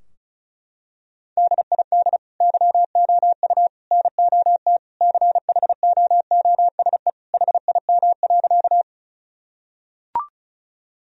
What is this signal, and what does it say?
Morse code, heard as a keyed tone, about 36 wpm, 700 Hz. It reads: DID YOU NOT CHOOSE HIM.
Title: Sentences from Top 1000 Words - Review Part 4 - 35wpm